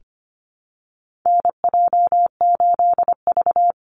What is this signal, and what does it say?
Morse code, heard as a keyed tone, about 25 wpm, 700 Hz. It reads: NJ8V
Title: Call Signs Repeated 2x - 25wpm